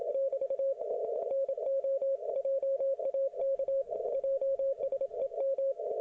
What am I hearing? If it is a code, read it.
Morse code, heard as a keyed tone, about 28 wpm, 540 Hz. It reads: <BT> NOJAK JSEM